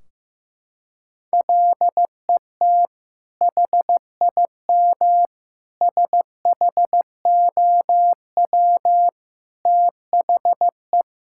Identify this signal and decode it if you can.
Morse code, heard as a keyed tone, about 15 wpm, 700 Hz. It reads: LET HIM SHOW THE
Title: Sentences from Top 1600 Words 15wpm